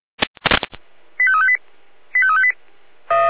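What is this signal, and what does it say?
Five-tone selective call (ZVEI), heard as quick runs of five tones, tones of about 70 ms. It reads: 85358 85358
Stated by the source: ZVEI five-tone selective call 85265, received on an SDR.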